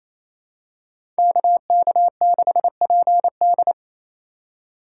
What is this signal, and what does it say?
Morse code, heard as a keyed tone, about 28 wpm, 700 Hz. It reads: KK6PD
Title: USA Call Signs - 5 Characters - 8x Character Spacing - 28wpm